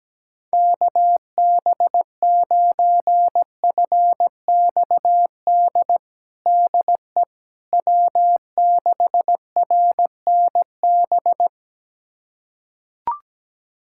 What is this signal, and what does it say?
Morse code, heard as a keyed tone, about 17 wpm, 700 Hz. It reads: KB9FXD DE W6RNB
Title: USA Callsign DE Callsign 17wpm